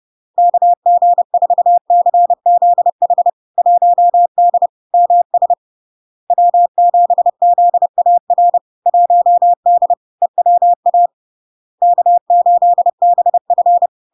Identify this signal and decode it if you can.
Morse code, heard as a keyed tone, about 30 wpm, 700 Hz. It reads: KG4CZH 1D MS W7ZAR 1D EWA K8BF
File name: LARK-FD-2022-Log-in-CW-30wpm-700-Hz